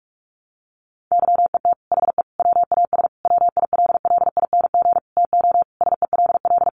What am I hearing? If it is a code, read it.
Morse code, heard as a keyed tone, about 46 wpm, 700 Hz. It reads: YET HE WAS WILLING TO SELL
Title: Sentences from Top 1100 Words 45wpm